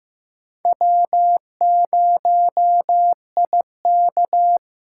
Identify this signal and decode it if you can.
Morse code, heard as a keyed tone, about 15 wpm, 700 Hz. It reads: W0IK
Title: USA Callsign DE Callsign - 4 Characters - 15wpm